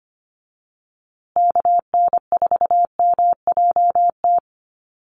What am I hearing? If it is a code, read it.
Morse code, heard as a keyed tone, about 25 wpm, 700 Hz. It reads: KN4MJT
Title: USA Call Signs Repeated 3x 25wpm